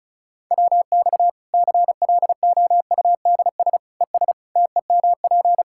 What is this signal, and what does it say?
Morse code, heard as a keyed tone, about 34 wpm, 700 Hz. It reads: WX CLOUDS ES TEMP
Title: Practice QSO - Exchange 2 - Info - 35wpm